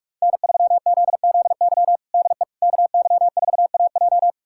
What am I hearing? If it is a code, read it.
Morse code, heard as a keyed tone, about 41 wpm, 700 Hz. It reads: N3ZZY DE KY4AJ
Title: USA Callsign DE Callsign - 5 Characters - 45wpm